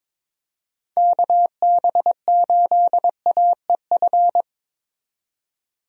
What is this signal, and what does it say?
Morse code, heard as a keyed tone, about 21 wpm, 700 Hz. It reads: KB8AEF
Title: USA Call Signs - 6 Characters - 22wpm